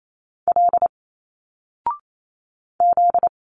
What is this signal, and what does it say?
Morse code, heard as a keyed tone, about 28 wpm, 700 Hz. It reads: L Z
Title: Single Letters 28wpm